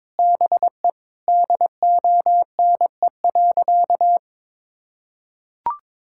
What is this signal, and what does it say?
Morse code, heard as a keyed tone, about 22 wpm, 700 Hz. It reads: BE DONE.
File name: Sentences with 9-Word Phrases - 22wpm